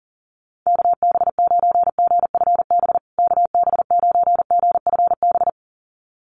Morse code, heard as a keyed tone, about 40 wpm, 700 Hz. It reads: KB9GFB KB9GFB
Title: USA Call Signs Repeated 2x 40wpm